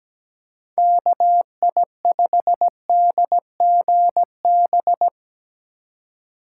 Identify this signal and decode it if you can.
Morse code, heard as a keyed tone, about 19 wpm, 700 Hz. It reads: KI5DGB
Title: Speed Racing - USA 2x3 Call Signs 17wpm